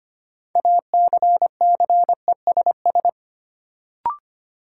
Morse code, heard as a keyed tone, about 25 wpm, 700 Hz. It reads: ACCESS
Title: Sentences from Top 1000 Words - Review Part 4 - New Word Drill - 25wpm